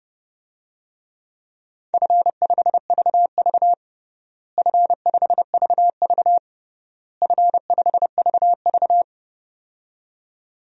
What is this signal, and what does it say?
Morse code, heard as a keyed tone, about 30 wpm, 700 Hz. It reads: F5VV F5VV F5VV
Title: DXCC-100 Call Signs - 4 Characters - Repeated 3x - 30wpm